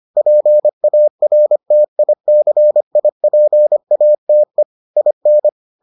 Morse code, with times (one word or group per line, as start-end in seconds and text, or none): 0.2-4.6 PARTICIPATE
5.0-5.5 IN